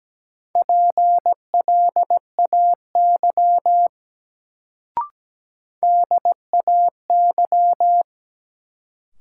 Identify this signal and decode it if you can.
Morse code, heard as a keyed tone, about 17 wpm, 700 Hz. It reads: PLAY DAY
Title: Sentences from Top 200 Words - New Word Drill - 17wpm